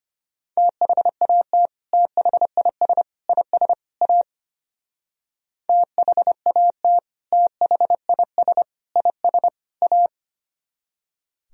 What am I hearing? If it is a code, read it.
Morse code, heard as a keyed tone, about 32 wpm, 700 Hz. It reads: THAT THIS IS A THAT THIS IS A